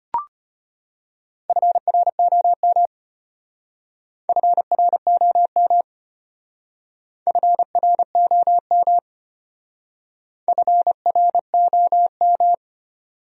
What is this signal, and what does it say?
Morse code, heard as a keyed tone, about 32 wpm, 700 Hz. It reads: FROM FROM FROM FROM